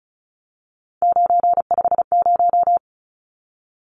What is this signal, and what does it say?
Morse code, heard as a keyed tone, about 34 wpm, 700 Hz. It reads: 950
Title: Numbers - 3-Digit 35wpm